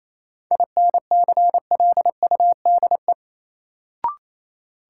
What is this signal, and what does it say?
Morse code, heard as a keyed tone, about 28 wpm, 700 Hz. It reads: INCLUDE
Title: Sentences from Top 1000 Words - Review Part 3 - New Word Drill - 28wpm